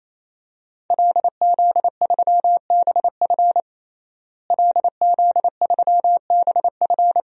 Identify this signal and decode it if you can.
Morse code, heard as a keyed tone, about 28 wpm, 700 Hz. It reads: LZ3BF LZ3BF
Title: DXCC-100 Call Signs - 5 Characters - Repeated 3x - 28wpm